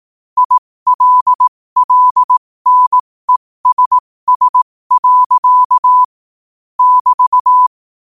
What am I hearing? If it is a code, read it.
Morse code, heard as a keyed tone, about 18 wpm, spaced as for 15 wpm, 1000 Hz. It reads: ILLNESS. <BT>
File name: qotdfiles / 15wpm